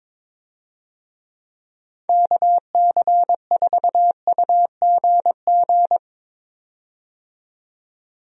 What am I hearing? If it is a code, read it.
Morse code, heard as a keyed tone, about 22 wpm, 700 Hz. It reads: KC4UGG
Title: USA Call Signs - 6 Characters - 22wpm